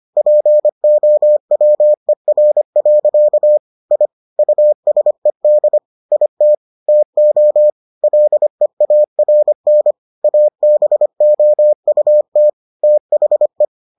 Morse code, heard as a keyed tone, about 25 wpm, 600 Hz. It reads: POWER. I USED IT TO LEARN ABOUT THE